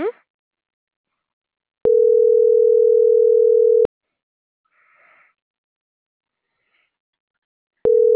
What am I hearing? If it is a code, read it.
Ringback tone (North America), 440 + 480 Hz.